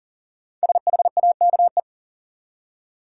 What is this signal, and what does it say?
Morse code, heard as a keyed tone, about 38 wpm, 700 Hz. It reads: SHAKE
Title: Five Letter Words - 40wpm